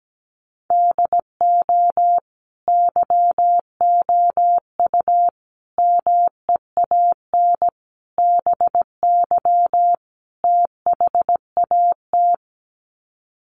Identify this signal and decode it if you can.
Morse code, heard as a keyed tone, about 17 wpm, 700 Hz. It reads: DO YOU MEAN BY THAT
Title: Sets of 5 Words Encore2 17wpm